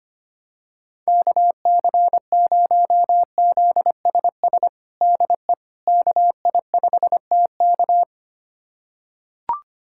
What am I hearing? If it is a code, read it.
Morse code, heard as a keyed tone, about 25 wpm, 700 Hz. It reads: KC0ZSS DE KI5TK
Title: USA Callsign DE Callsign 25wpm